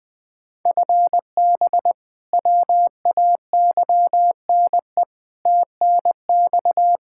Morse code, heard as a keyed tone, about 20 wpm, 700 Hz. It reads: FB WAYNE TNX
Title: Practice QSO - Exchange 3 - More Info - 20wpm